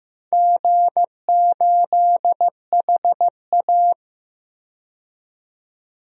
Morse code, heard as a keyed tone, about 15 wpm, 700 Hz. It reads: G8HA
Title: DXCC-100 Call Signs - 4 Characters - 15wpm